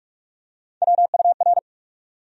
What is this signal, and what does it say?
Morse code, heard as a keyed tone, about 47 wpm, 700 Hz. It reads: WUR
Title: USA Call Sign Postfixes - 3 Letters 45wpm